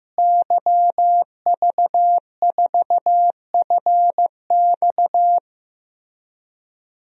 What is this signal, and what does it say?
Morse code, heard as a keyed tone, about 15 wpm, 700 Hz. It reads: YV4FX
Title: DXCC-100 Call Signs - 5 Characters - Repeated 3x - 15wpm